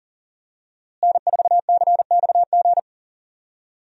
Morse code, heard as a keyed tone, about 40 wpm, 700 Hz. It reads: N4CXG
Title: USA Call Signs - 5 Characters - Repeated 3x - 40wpm